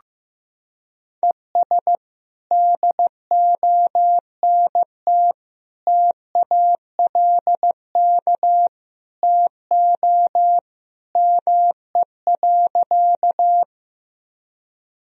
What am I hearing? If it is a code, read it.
Morse code, heard as a keyed tone, about 15 wpm, 700 Hz. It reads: ES DONT TALK TO ME.
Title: Sentences from Top 400 Words and Abbreviations 15wpm